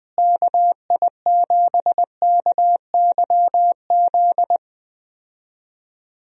Morse code, heard as a keyed tone, about 20 wpm, 700 Hz. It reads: KI7KYZ
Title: USA Call Signs - 6 Characters - Repeated 2x - 20wpm